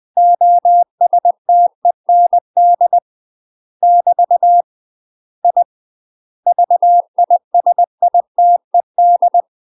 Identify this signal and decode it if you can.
Morse code, heard as a keyed tone, about 20 wpm, 700 Hz. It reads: OSTEND <BT> I VISITED
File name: UK MASTER QSOs TRAINING / 20WPM / UK MASTER QSOs 0065